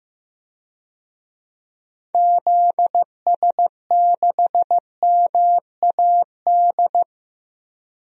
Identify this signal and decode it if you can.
Morse code, heard as a keyed tone, about 15 wpm, 700 Hz. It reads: ZS6MAD
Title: DXCC-100 Call Signs - 6 Characters - Repeated 2x - 15wpm